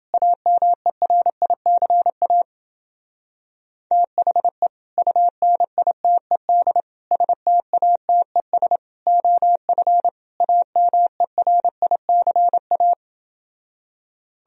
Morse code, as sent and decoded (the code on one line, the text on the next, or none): .- -- . .-. .. -.-. .- / - .... / ..- -. .. - . -.. / ... - .- - . ... / --- ..-. / .- -- . .-. .. -.-. .-
AMERICA THE UNITED STATES OF AMERICA